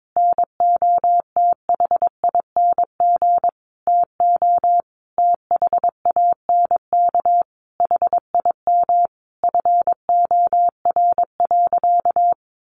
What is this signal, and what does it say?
Morse code, heard as a keyed tone, about 22 wpm, 700 Hz. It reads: NOTHING TO THANK HIM FOR.
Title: Sentences from Top 1000 Words 22wpm